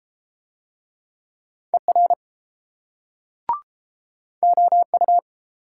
Morse code, heard as a keyed tone, about 29 wpm, 700 Hz. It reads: ER OU